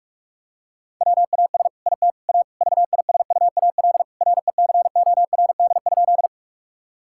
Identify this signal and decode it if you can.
Morse code, heard as a keyed tone, about 46 wpm, 700 Hz. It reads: WAS IT A VISUAL RECORD?
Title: Sentences from Top 1000 Words - Review Part 6 - 45wpm